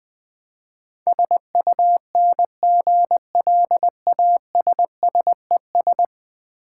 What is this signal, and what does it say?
Morse code, heard as a keyed tone, about 20 wpm, 700 Hz. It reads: SUNGLASSES
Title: Easy Words with Hints - 20wpm